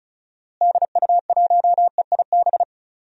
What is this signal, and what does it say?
Morse code, heard as a keyed tone, about 36 wpm, 700 Hz. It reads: DU1EIB
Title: DXCC-100 Call Signs - 6 Characters - Repeated 3x - 35wpm